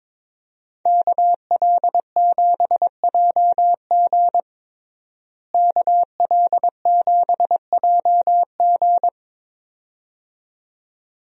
Morse code, heard as a keyed tone, about 22 wpm, 700 Hz. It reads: KL7JG KL7JG
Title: DXCC-100 Call Signs - 5 Characters - Repeated 2x - 22wpm